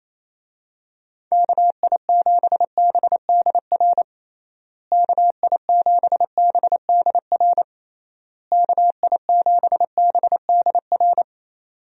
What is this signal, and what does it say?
Morse code, heard as a keyed tone, about 28 wpm, 700 Hz. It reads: KI7BDR KI7BDR KI7BDR
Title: USA Call Signs - 6 Characters - Repeated 3x - 28wpm